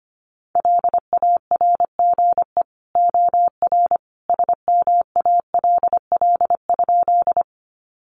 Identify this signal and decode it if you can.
Morse code, heard as a keyed tone, about 25 wpm, 700 Hz. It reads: LARGE OR SMALL?
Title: Sentences from Top 200 Words 25wpm